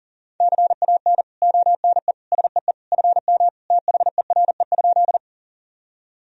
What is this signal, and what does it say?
Morse code, heard as a keyed tone, about 40 wpm, 700 Hz. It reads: CAN ONE SEE FM THERE?
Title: Sentences from Top 100 Words and Abbreviations 40wpm